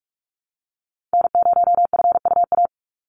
Morse code, heard as a keyed tone, about 44 wpm, 700 Hz. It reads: N0FUA
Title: USA Call Signs 45wpm